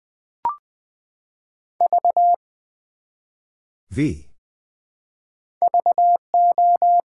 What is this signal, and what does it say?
Morse code, heard as a keyed tone, about 20 wpm, 700 Hz. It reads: V VO
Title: Build A Word - 1 Letter At A Time - 3-Letter Words - 20wpm